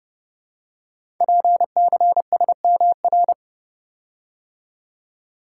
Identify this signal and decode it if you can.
Morse code, heard as a keyed tone, about 30 wpm, 700 Hz. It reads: PCSMR
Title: Code Groups - 5 Letter:Numbers - 30wpm